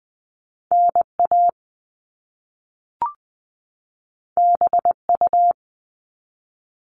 Morse code, heard as a keyed tone, about 20 wpm, 700 Hz. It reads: NA BU
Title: All Cut Numbers - 2-Digit 20wpm